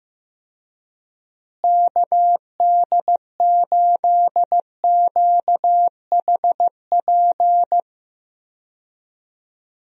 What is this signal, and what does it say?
Morse code, heard as a keyed tone, about 15 wpm, 700 Hz. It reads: KD8QHP
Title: USA Call Signs - 6 Characters - Repeated 2x - 15wpm